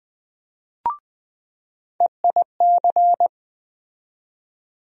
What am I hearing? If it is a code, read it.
Morse code, heard as a keyed tone, about 20 wpm, 700 Hz. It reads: EIC
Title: USA Call Sign Postfixes - 3 Letters 20wpm